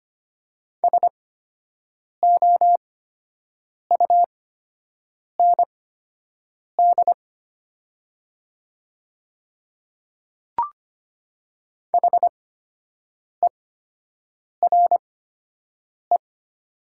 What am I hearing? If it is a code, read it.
Morse code, heard as a keyed tone, about 25 wpm, spaced as for 7 wpm, 700 Hz. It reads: SOUND HERE
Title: 8x Character Spacing - Top 200 Words - 25wpm